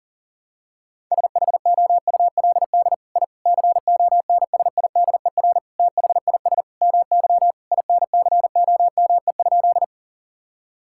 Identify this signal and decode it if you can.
Morse code, heard as a keyed tone, about 40 wpm, 700 Hz. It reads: SHOULD I CONSIDER THIS MY INCOME?
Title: Sentences from Top 1000 Words - Review Part 5 - 40wpm